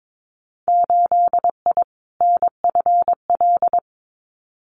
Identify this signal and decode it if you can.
Morse code, heard as a keyed tone, about 22 wpm, 700 Hz. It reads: 8I NFL
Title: Winter Field Day Exchanges Encore 22wpm